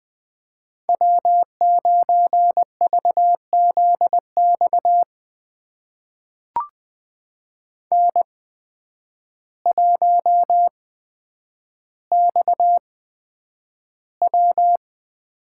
Morse code, heard as a keyed tone, about 21 wpm, spaced as for 9 wpm, 700 Hz. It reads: W9VZX N1XW
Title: USA Call Signs - 8x Character Spacing - 20wpm